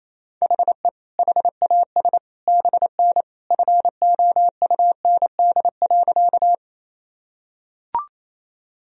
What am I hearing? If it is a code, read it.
Morse code, heard as a keyed tone, about 28 wpm, 700 Hz. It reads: HE HAS BN FOUND.
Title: Sentences from Top 300 Words and Abbreviations 28wpm